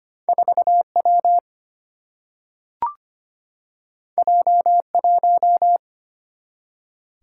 Morse code, heard as a keyed tone, about 25 wpm, 700 Hz. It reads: 4 W J 1
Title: Single-Letter Single-Number 25wpm